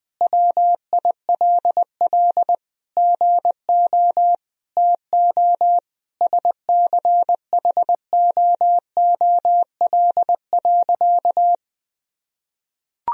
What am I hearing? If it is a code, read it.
Morse code, heard as a keyed tone, about 20 wpm, 700 Hz. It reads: WILL GO TO SCHOOL.
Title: Sentences from Top 1000 Words - Review Part 1 - 20wpm